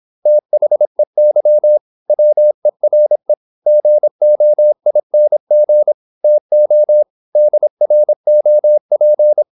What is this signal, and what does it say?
Morse code, heard as a keyed tone, about 26 wpm, 600 Hz. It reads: THEY WERE GOING TO DROP
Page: Never 26wpm | CW med Gnister